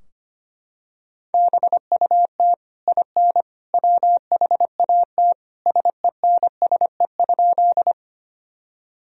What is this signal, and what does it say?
Morse code, heard as a keyed tone, about 25 wpm, 700 Hz. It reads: BUT IN WHAT SENSE?
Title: Sentences from Top 1000 Words - Review Part 4 - 25wpm